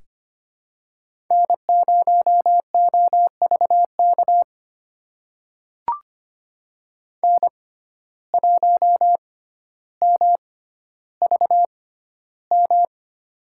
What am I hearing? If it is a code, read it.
Morse code, heard as a keyed tone, about 25 wpm, spaced as for 13 wpm, 700 Hz. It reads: N0OVK N1MVM
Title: USA Call Signs - 5 Characters - 6x Character Spacing - 25wpm